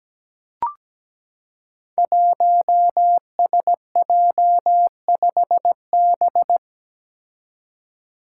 Morse code, heard as a keyed tone, about 17 wpm, 700 Hz. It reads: 1SJ5B
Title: Five Challenging Letter-Number Combinations 17wpm